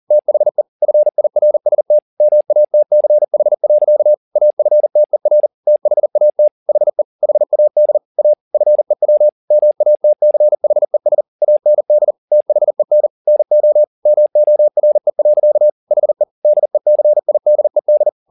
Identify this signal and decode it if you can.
Morse code, heard as a keyed tone, about 40 wpm, 600 Hz. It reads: THE FIRST MATCH. AFTER THAT HE HAD A FEW MATCHES AND THEN NO MORE. HE DECIDED
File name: the old man and the sea 2 40wpm